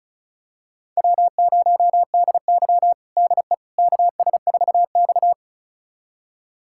Morse code, heard as a keyed tone, about 36 wpm, 700 Hz. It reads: W0DY DE KS4X
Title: USA Callsign DE Callsign - 4 Characters - 35wpm